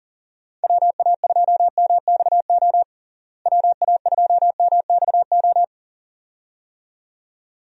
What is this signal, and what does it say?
Morse code, heard as a keyed tone, about 40 wpm, 700 Hz. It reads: WA2MXO WA2MXO
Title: USA Call Signs - 6 Characters - Repeated 2x - 40wpm